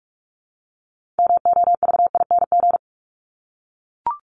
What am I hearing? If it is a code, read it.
Morse code, heard as a keyed tone, about 44 wpm, 700 Hz. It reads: MOVING
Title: ING Suffix - Top 100 Words with Base Words 45wpm